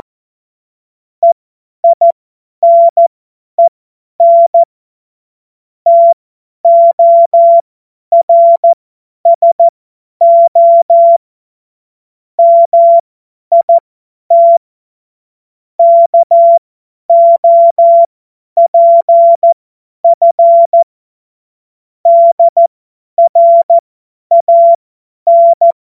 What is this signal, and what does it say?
Morse code, heard as a keyed tone, about 14 wpm, spaced as for 10 wpm, 680 Hz. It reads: EINEN TORSO MIT KOPF DRAN
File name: Morsekurs / Witze / Witz 01. August 2023 50 F70